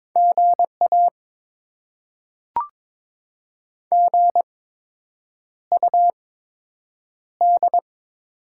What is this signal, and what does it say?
Morse code, heard as a keyed tone, about 22 wpm, spaced as for 8 wpm, 700 Hz. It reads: GA GUD
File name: Top 20 QSO Elements - 8x Character Spacing - 22wpm